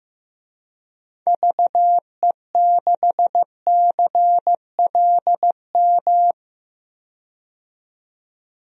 Morse code, heard as a keyed tone, about 16 wpm, 700 Hz. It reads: VE6CLM
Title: DXCC-100 Call Signs - 6 Characters - 15wpm